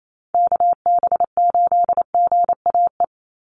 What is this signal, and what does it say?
Morse code, heard as a keyed tone, about 28 wpm, 700 Hz. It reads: KB8GAE
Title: Call Signs Repeated 2x - 28wpm